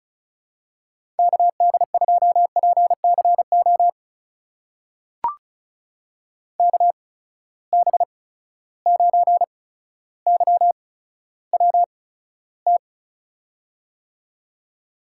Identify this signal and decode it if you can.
Morse code, heard as a keyed tone, about 34 wpm, spaced as for 15 wpm, 700 Hz. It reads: KD2PCO KB9YWT
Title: USA Call Signs - 6 Characters - 8x Character Spacing - 35wpm